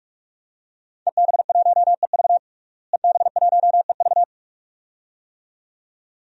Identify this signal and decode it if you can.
Morse code, heard as a keyed tone, about 45 wpm, 700 Hz. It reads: EB1EV EB1EV
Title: DXCC-100 Call Signs - 5 Characters - Repeated 2x - 45wpm